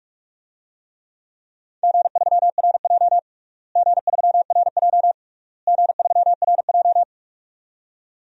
Morse code, heard as a keyed tone, about 46 wpm, 700 Hz. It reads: G3RJ G3RJ G3RJ
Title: DXCC-100 Call Signs - 4 Characters - Repeated 3x - 45wpm